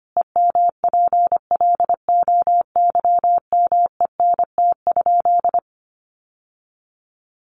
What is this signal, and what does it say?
Morse code, heard as a keyed tone, about 25 wpm, 700 Hz. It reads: EMPLOYMENT?
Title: Sentences from Top 1500 Words 25wpm